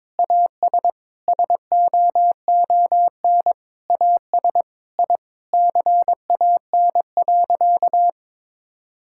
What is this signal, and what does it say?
Morse code, heard as a keyed tone, about 22 wpm, 700 Hz. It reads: AS SOON AS I CAN.
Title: Sentences from Top 1000 Words - Review Part 5 - 22wpm